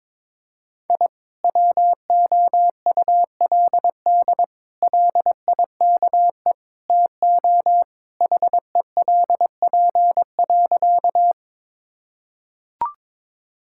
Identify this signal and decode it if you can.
Morse code, heard as a keyed tone, about 22 wpm, 700 Hz. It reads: I WOULD LIKE TO HELP.
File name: practice / 22wpm / Sentences with 4-Word Phrases - 22wpm